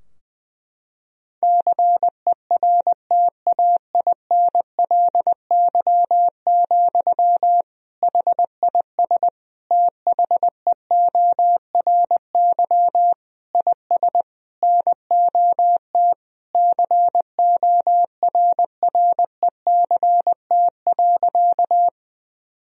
Morse code, heard as a keyed tone, about 21 wpm, 700 Hz. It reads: CERTAINLY, HIS THEORY IS NOT CORRECT.
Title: Sentences from Top 1000 Words - Review Part 3 - 20wpm